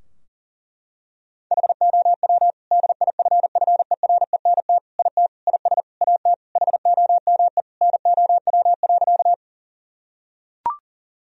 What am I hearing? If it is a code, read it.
Morse code, heard as a keyed tone, about 40 wpm, 700 Hz. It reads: HOW DIFFERENT IT IS AT HOME NOW.
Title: Sentences from Top 1000 Words - Review Part 3 - 40wpm